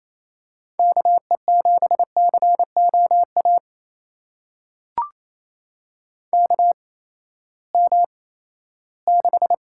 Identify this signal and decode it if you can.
Morse code, heard as a keyed tone, about 28 wpm, spaced as for 16 wpm, 700 Hz. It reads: KE7COA KM6
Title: USA Call Signs - 6 Characters - 8x Character Spacing - 28wpm